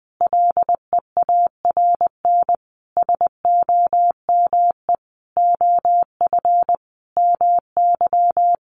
Morse code, heard as a keyed tone, about 20 wpm, 700 Hz. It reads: LEARN SOME OF MY
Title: Sentences from Top 900 Words 20wpm